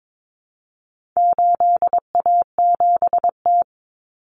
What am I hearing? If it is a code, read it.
Morse code, heard as a keyed tone, about 22 wpm, 700 Hz. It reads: 8A7T
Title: Common Cut Numbers - 4-Digit 22wpm